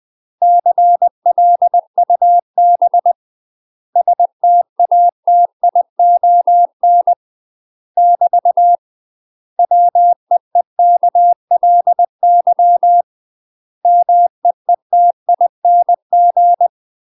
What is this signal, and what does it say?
Morse code, heard as a keyed tone, about 20 wpm, 700 Hz. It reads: CLUB STATION <BT> WEEKLY MEETING
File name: UK MASTER QSOs TRAINING / 20WPM / UK MASTER QSOs 0050